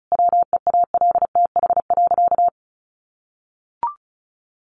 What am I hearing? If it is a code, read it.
Morse code, heard as a keyed tone, about 34 wpm, 700 Hz. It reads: WEALTH.
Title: Sentences from Top 1200 Words 35wpm